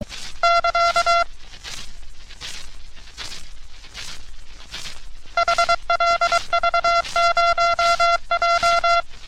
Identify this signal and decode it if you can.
Morse code, heard as a keyed tone, about 23 wpm, 1370 Hz. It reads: <KA> HLV0J
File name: U.S. Navy - RN-2849-124 - Sixteens Morse Code Practice